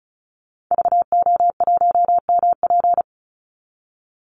Morse code, heard as a keyed tone, about 34 wpm, 700 Hz. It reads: VO1MP
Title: Call Signs Repeated 3x - 35wpm